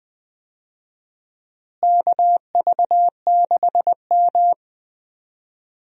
Morse code, heard as a keyed tone, about 20 wpm, 700 Hz. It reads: KV6M
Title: USA Call Signs - 4 Characters - Repeated 3x - 20wpm